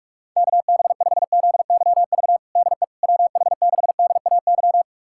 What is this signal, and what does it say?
Morse code, heard as a keyed tone, about 46 wpm, 700 Hz. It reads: KB5ZYV DE WH6DAY
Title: USA Callsign DE Callsign - 6 Characters - 45wpm